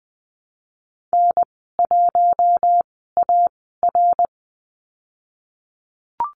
Morse code, heard as a keyed tone, about 20 wpm, spaced as for 14 wpm, 700 Hz. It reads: N1AR
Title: 062 Lesson 11g - Letter h - Call Signs 20wpm